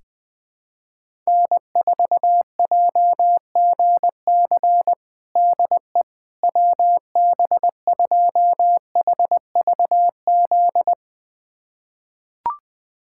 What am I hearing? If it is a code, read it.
Morse code, heard as a keyed tone, about 20 wpm, 700 Hz. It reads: N4JGC DE WB2HVZ
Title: USA Callsign DE Callsign 20wpm